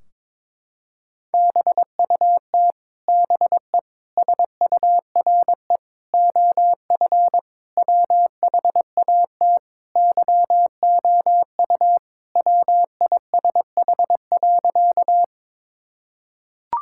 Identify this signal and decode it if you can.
Morse code, heard as a keyed tone, about 22 wpm, 700 Hz. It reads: BUT BE SURE OF WHAT YOU WISH.
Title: Sentences from Top 1600 Words 22wpm